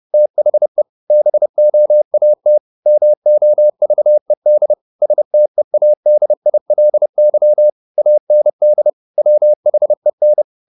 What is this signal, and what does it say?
Morse code, heard as a keyed tone, about 30 wpm, 600 Hz. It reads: THE BOAT MOVED STEADILY AND WHEN
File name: the old man and the sea / the old man and the sea 1 30wpm